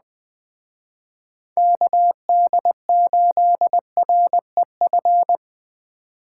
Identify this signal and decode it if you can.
Morse code, heard as a keyed tone, about 20 wpm, 700 Hz. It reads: KD8REF